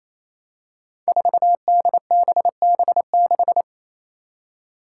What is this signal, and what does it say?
Morse code, heard as a keyed tone, about 28 wpm, 700 Hz. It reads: 4DBB6
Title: Five Challenging Letter-Number Combinations 28wpm